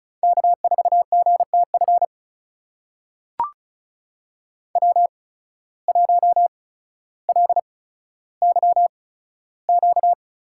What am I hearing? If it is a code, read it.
Morse code, heard as a keyed tone, about 36 wpm, spaced as for 15 wpm, 700 Hz. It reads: K4GTF W1LYQ
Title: USA Call Signs - 5 Characters - 8x Character Spacing - 35wpm